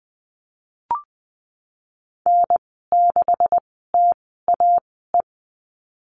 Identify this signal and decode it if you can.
Morse code, heard as a keyed tone, about 20 wpm, spaced as for 14 wpm, 700 Hz. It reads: N6TAE
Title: 154 Lesson 25g - Letter p - Call Signs 20wpm